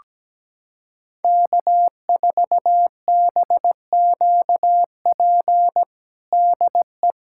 Morse code, heard as a keyed tone, about 17 wpm, 700 Hz. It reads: K4BQP DE
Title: USA Callsign DE Callsign 17wpm